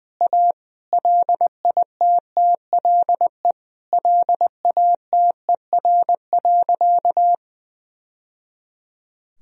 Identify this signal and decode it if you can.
Morse code, heard as a keyed tone, about 21 wpm, 700 Hz. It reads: A LITTLE LATER.
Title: Sentences from Top 1000 Words - Review Part 3 - 20wpm